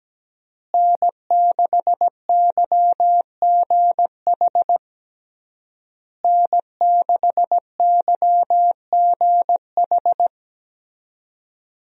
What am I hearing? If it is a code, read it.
Morse code, heard as a keyed tone, about 17 wpm, 700 Hz. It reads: N6YGH N6YGH
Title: USA Call Signs - 5 Characters - Repeated 3x - 17wpm